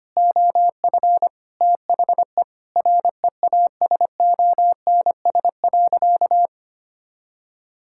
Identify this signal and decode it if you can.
Morse code, heard as a keyed tone, about 25 wpm, 700 Hz. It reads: OF THE REASONS.
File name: practice / 25wpm / Sentences with 4-Word Phrases - 25wpm